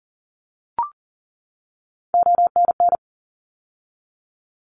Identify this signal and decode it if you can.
Morse code, heard as a keyed tone, about 37 wpm, 700 Hz. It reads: ONN
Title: ARRL Sections 40wpm